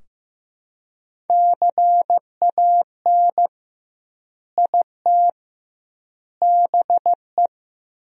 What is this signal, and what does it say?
Morse code, heard as a keyed tone, about 16 wpm, 700 Hz. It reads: CAN IT BE
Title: 2x Word-Spacing - Sentences from Top 500 Words - 15wpm